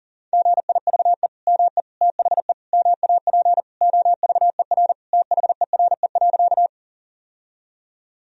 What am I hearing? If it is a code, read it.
Morse code, heard as a keyed tone, about 40 wpm, 700 Hz. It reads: GIVE ME THE MAP OVER THERE.
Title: Sentences from Top 1000 Words - Review Part 2 - 40wpm